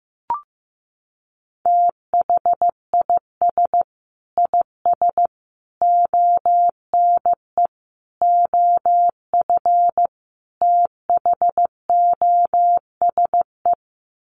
Morse code, heard as a keyed tone, about 15 wpm, 700 Hz. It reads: THIS IS ONE OF THOSE
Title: Sets of 5 Words Encore2 15wpm